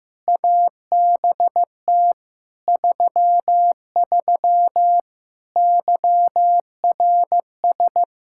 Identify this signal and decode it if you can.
Morse code, heard as a keyed tone, about 16 wpm, 700 Hz. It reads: ABT 33 YRS
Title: K1USN SST QSO Practice 15wpm